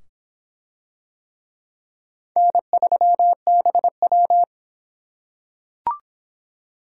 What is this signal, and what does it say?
Morse code, heard as a keyed tone, about 23 wpm, 700 Hz. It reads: N3BW